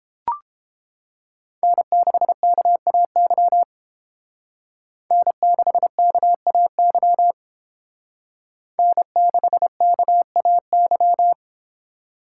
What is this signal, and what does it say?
Morse code, heard as a keyed tone, about 30 wpm, 700 Hz. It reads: N6KAY N6KAY N6KAY